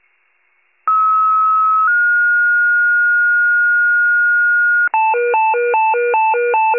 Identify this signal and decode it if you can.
Two-tone page, heard as a one-second tone, then a three-second tone, about 1290 then 1430 Hz. It is heard from 0.9 to 4.9 s.
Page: County Wide Paging Archive